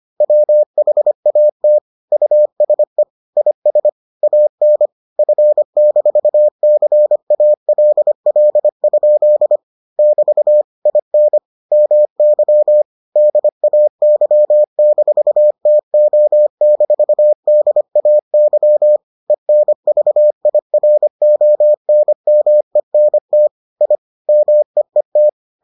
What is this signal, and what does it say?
Morse code, heard as a keyed tone, about 25 wpm, 600 Hz. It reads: WHAT USE IS AN F-CALL? <BT> IN MY DAY-TO-DAY ENVIRONMENT I MEET